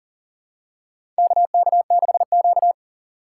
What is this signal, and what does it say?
Morse code, heard as a keyed tone, about 40 wpm, 700 Hz. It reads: KK6Q
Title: USA Call Signs - 4 Characters - 3x Character Spacing - 40wpm